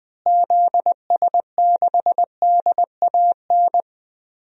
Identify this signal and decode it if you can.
Morse code, heard as a keyed tone, about 20 wpm, 700 Hz. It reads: ZS6DAN
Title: DXCC-100 Call Signs - 6 Characters - Repeated 3x - 20wpm